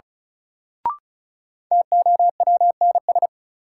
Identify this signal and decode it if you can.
Morse code, heard as a keyed tone, about 36 wpm, 700 Hz. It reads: TOWNS
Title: Sentences from Top 1000 Words - Review Part 6 - New Word Drill - 35wpm